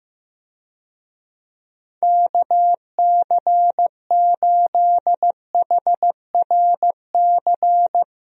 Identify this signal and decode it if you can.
Morse code, heard as a keyed tone, about 15 wpm, 700 Hz. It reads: KC8HRC
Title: USA Call Signs - 6 Characters - Repeated 3x - 15wpm